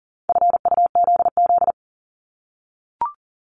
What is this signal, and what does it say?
Morse code, heard as a keyed tone, about 38 wpm, 700 Hz. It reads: FUZZ
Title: Multi-letter Phonemes 40wpm